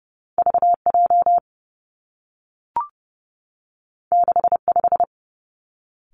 Morse code, heard as a keyed tone, about 30 wpm, 700 Hz. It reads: V J 6 5
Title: Two Challenging Letter-Number Combinations 30wpm